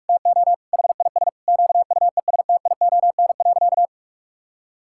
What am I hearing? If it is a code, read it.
Morse code, heard as a keyed tone, about 44 wpm, 700 Hz. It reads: TO HIS QUESTION.
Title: Sentences from Top 1000 Words - Review Part 6 - 45wpm